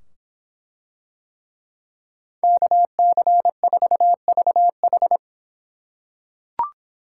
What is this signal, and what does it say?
Morse code, heard as a keyed tone, about 25 wpm, 700 Hz. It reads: KC4VH